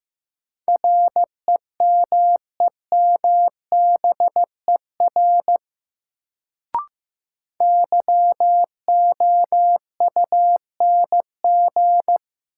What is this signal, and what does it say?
Morse code, heard as a keyed tone, about 15 wpm, 700 Hz. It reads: REMEMBER YOUNG
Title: Sentences from Top 400 Words - New Word Drill - 15wpm